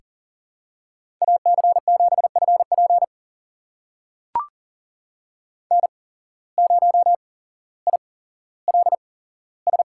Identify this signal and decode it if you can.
Morse code, heard as a keyed tone, about 40 wpm, spaced as for 17 wpm, 700 Hz. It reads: AC7FP N0ILS
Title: USA Call Signs - 5 Characters - 8x Character Spacing - 40wpm